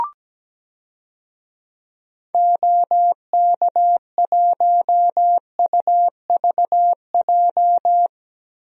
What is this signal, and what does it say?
Morse code, heard as a keyed tone, about 17 wpm, 700 Hz. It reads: OK1UVJ
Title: DXCC-100 Call Signs - 6 Characters - Repeated 2x - 17wpm